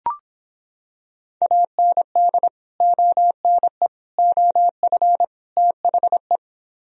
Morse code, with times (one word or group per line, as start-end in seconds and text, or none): 1.4-2.5 AND
2.8-3.9 ONE
4.2-5.2 OF
5.6-6.4 THE